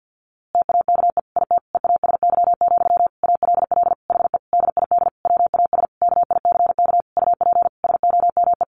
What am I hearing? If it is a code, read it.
Morse code, heard as a keyed tone, about 50 wpm, 700 Hz. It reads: TAKE IT EASY, ALL HE DID WAS KICK UP SOME
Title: A Sound Of Thunder 50wpm